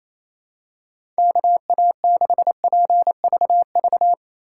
Morse code, heard as a keyed tone, about 28 wpm, 700 Hz. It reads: KA6PVV
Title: USA Call Signs - 6 Characters - 2x Character Spacing - 28wpm